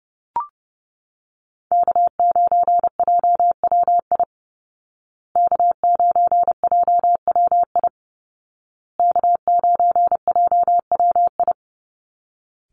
Morse code, heard as a keyed tone, about 28 wpm, 700 Hz. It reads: K9JWI K9JWI K9JWI
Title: Call Signs Repeated 2x - 30wpm